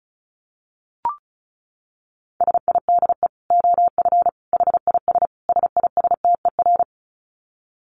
Morse code, heard as a keyed tone, about 34 wpm, 700 Hz. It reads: SIDE OF HIS SISTER
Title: Dit Challenge 35wpm